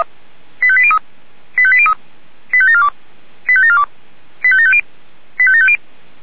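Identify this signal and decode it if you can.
Five-tone selective call (ZVEI), heard as quick runs of five tones, tones of about 70 ms. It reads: 85803 85803 85742 85742 85750 85750
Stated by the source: Five-tone ZVEI selective call 85724 received by SDR.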